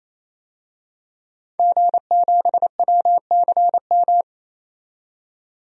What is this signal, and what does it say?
Morse code, heard as a keyed tone, about 28 wpm, 700 Hz. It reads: G7WCM
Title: DXCC-100 Call Signs - 5 Characters - 28wpm